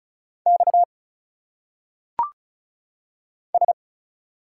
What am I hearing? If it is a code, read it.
Morse code, heard as a keyed tone, about 36 wpm, 700 Hz. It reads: X S
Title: Single Letters - Phonetic Pronunciation 35wpm